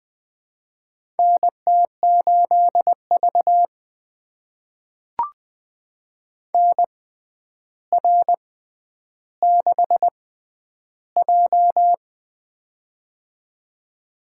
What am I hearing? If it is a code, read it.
Morse code, heard as a keyed tone, about 20 wpm, spaced as for 10 wpm, 700 Hz. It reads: NT8V NR6J
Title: USA Call Signs - 4 Characters - 6x Character Spacing - 20wpm